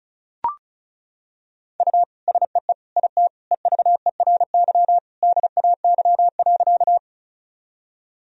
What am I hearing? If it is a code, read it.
Morse code, heard as a keyed tone, about 34 wpm, 690 Hz. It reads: U SEE IT EVERY DAY.
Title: Sentences from Top 200 Words and Abbreviations 35wpm